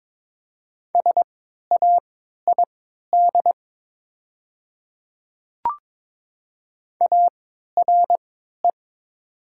Morse code, heard as a keyed tone, about 22 wpm, spaced as for 13 wpm, 700 Hz. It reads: SAID ARE
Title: 3x Character Spacing - Top 300 Words - 22wpm